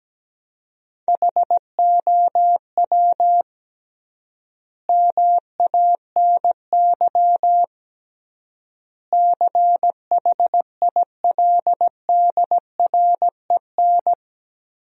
Morse code, heard as a keyed tone, about 17 wpm, 700 Hz. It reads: HOW MANY CHILDREN
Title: 3x Word-Spacing - Sentences from Top 400 Words - 17wpm